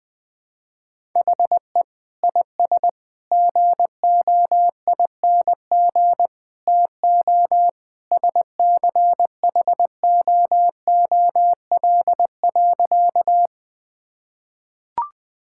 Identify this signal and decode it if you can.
Morse code, heard as a keyed tone, about 21 wpm, 700 Hz. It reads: HE IS GOING TO SCHOOL.
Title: Sentences with 3-Word Phrases - Spoken Beginning - 20wpm